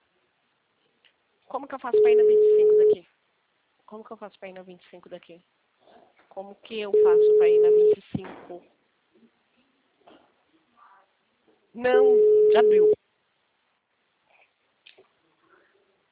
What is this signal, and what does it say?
Ringback tone (Europe, 425 Hz), with speech.